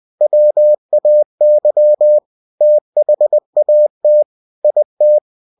Morse code, heard as a keyed tone, about 20 wpm, 600 Hz. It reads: WAY THAT IT
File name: cw-books / winnie-the-pooh / winnie-20wpm-page-0102